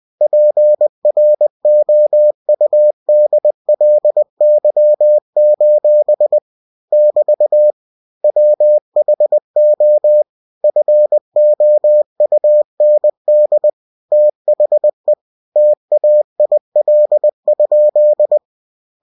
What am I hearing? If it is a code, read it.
Morse code, heard as a keyed tone, about 20 wpm, 600 Hz. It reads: PROUDLY: <BT> WHO FOUND THE TAIL?
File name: cw-books / winnie-the-pooh / winnie-20wpm-page-0032